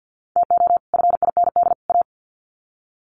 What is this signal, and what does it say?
Morse code, heard as a keyed tone, about 50 wpm, 700 Hz. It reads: TO FIND A
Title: Sets of 3 Words 50wpm